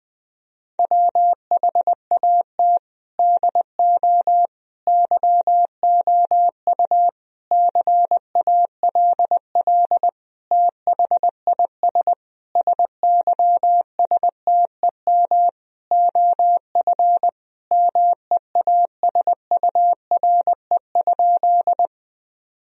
Morse code, heard as a keyed tone, about 21 wpm, 700 Hz. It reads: WHAT DO YOU CALL THIS SYSTEM OF MEASURE?
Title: Sentences from Top 1000 Words - Review Part 3 - 20wpm